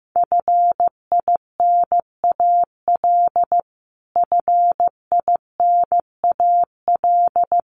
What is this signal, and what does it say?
Morse code, heard as a keyed tone, about 15 wpm, 700 Hz. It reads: FINAL FINAL
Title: Top 500 Words Repeated 15wpm